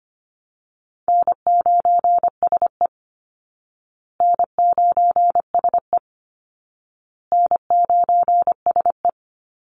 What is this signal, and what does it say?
Morse code, heard as a keyed tone, about 25 wpm, 700 Hz. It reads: N9SE N9SE N9SE
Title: Easy Call Signs Repeated 3x - 25wpm